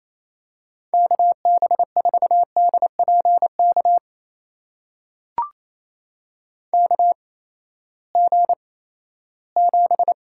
Morse code, heard as a keyed tone, about 28 wpm, spaced as for 16 wpm, 700 Hz. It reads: KB4DPK KG7
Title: USA Call Signs - 6 Characters - 8x Character Spacing - 28wpm